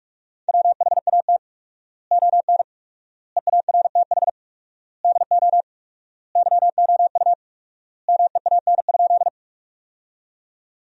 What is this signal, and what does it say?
Morse code, heard as a keyed tone, about 41 wpm, 700 Hz. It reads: WHAT ON EARTH DO YOU MEAN?
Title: 4x Word-Spacing - Sentences from Top 200 Words - 45wpm